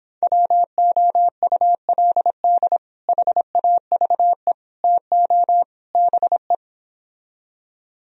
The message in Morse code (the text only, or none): WOULD HAVE TO BE